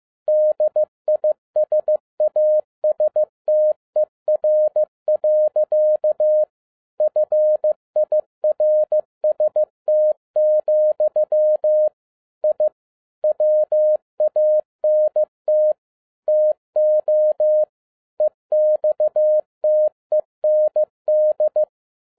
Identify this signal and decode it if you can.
Morse code, heard as a keyed tone, about 15 wpm, 610 Hz. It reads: DISASTER. FIRST, I WANT TO EXTEND
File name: 2011-07-mp3 / n37 THE 3.11 2 15wpm